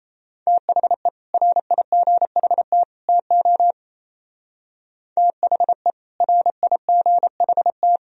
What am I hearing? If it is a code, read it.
Morse code, heard as a keyed tone, about 33 wpm, 700 Hz. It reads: THE RIGHT TO THE RIGHT
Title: Speed-Racing - Sets of 3 Words 28wpm